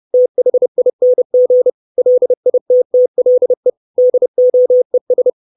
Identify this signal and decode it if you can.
Morse code, heard as a keyed tone, about 30 wpm, 500 Hz. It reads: THING LITTLE DOES
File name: 632 Words 30WPM